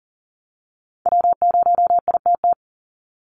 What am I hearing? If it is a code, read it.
Morse code, heard as a keyed tone, about 40 wpm, 700 Hz. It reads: W0ITT
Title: Call Signs Repeated 3x - 40wpm